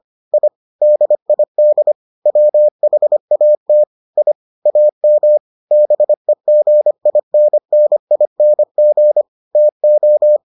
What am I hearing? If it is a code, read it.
Morse code, heard as a keyed tone, about 25 wpm, 600 Hz. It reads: I DID WHAT I AM BEGINNING TO